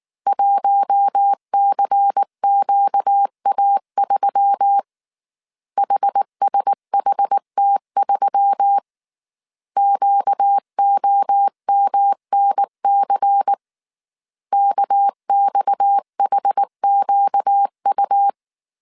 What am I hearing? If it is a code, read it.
Morse code, heard as a keyed tone, about 19 wpm, 800 Hz. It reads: 1CQA3 HSHT3 QOMNC KXHQU
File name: CW / DET / 1 6 1500C S A 18wpm 2255